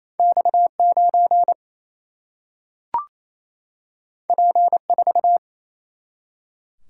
Morse code, heard as a keyed tone, about 28 wpm, 700 Hz. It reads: X9 P4
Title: Single-Letter Single-Number 28wpm